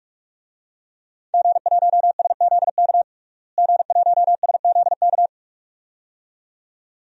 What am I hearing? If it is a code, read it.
Morse code, heard as a keyed tone, about 42 wpm, 700 Hz. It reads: G1SZK G1SZK
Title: DXCC-100 Call Signs - 5 Characters - Repeated 2x - 45wpm